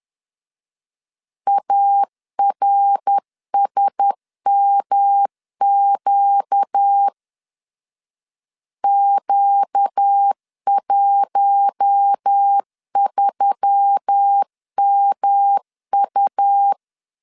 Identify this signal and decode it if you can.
Morse code, heard as a keyed tone, about 10 wpm, 800 Hz. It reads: ARSMQ Q13MU
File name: LEZIONI / 6 PKX0 / 6 PK 10wpm 2436